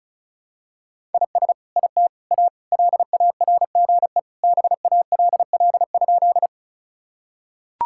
Morse code, heard as a keyed tone, about 34 wpm, 700 Hz. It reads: IS IT A LARGE BALL?
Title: Sentences from Top 1000 Words - Review Part 6 - 35wpm